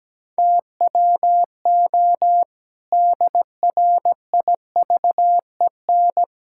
Morse code, heard as a keyed tone, about 17 wpm, 700 Hz. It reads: TWO DRIVEN
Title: Pangram Challenge v2 - 17wpm